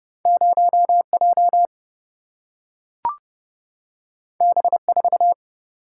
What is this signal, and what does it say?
Morse code, heard as a keyed tone, about 30 wpm, 700 Hz. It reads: 0J B4
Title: Single-Letter Single-Number 30wpm